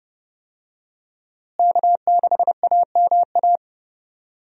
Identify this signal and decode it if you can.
Morse code, heard as a keyed tone, about 31 wpm, 700 Hz. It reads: K6AMA
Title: USA Call Signs - 5 Characters - Repeated 2x - 30wpm